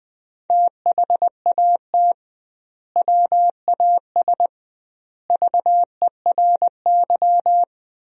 Morse code, heard as a keyed tone, about 20 wpm, 700 Hz. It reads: THAT WAS VERY
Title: 2x Word-Spacing - Sentences from Top 200 Words - 20wpm